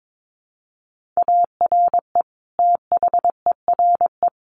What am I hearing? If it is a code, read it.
Morse code, heard as a keyed tone, about 22 wpm, 700 Hz. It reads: ARE THERE
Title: Sentences from Top 1400 Words 22wpm